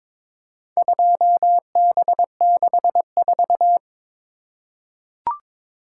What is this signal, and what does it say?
Morse code, heard as a keyed tone, about 22 wpm, 700 Hz. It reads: 2B64
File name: practice / 22wpm / Four Letter-Number Combinations - 22wpm